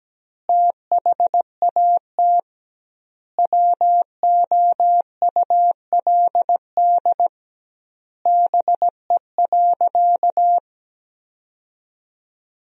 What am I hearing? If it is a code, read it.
Morse code, heard as a keyed tone, about 17 wpm, 700 Hz. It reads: THAT WOULD BE.
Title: 2x Word-Spacing - Sentences from Top 200 Words - 17wpm